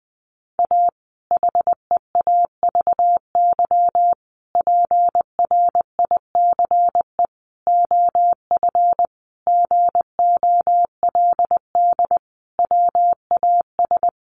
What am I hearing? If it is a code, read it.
Morse code, heard as a keyed tone, about 20 wpm, 700 Hz. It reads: A HEAVY PRICE OF GOLD WAS
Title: Sentences from Top 1100 Words 20wpm